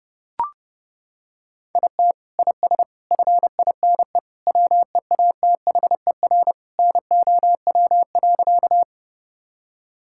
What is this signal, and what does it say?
Morse code, heard as a keyed tone, about 30 wpm, 700 Hz. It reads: IT IS FINE WEATHER NOW.
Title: Sentences from Top 1000 Words - Review Part 5 - 30wpm